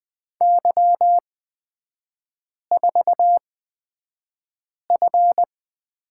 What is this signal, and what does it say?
Morse code, heard as a keyed tone, about 20 wpm, spaced as for 5 wpm, 690 Hz. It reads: Y4F
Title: 255 Lesson 41e - Period - Three Characters 20wpm